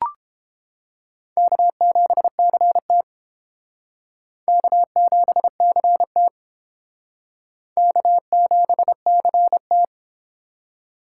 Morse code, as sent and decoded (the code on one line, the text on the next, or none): -.- --... -.-. - / -.- --... -.-. - / -.- --... -.-. -
K7CT K7CT K7CT